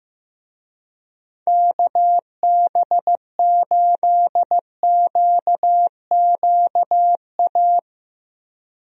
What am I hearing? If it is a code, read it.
Morse code, heard as a keyed tone, about 15 wpm, 700 Hz. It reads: KB8QQA
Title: USA Call Signs - 6 Characters - Repeated 2x - 15wpm